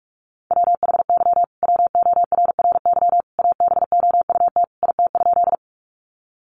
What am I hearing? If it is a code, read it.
Morse code, heard as a keyed tone, about 44 wpm, 700 Hz. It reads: WHY WORRY ABOUT IT?
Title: Sentences from Top 1100 Words 45wpm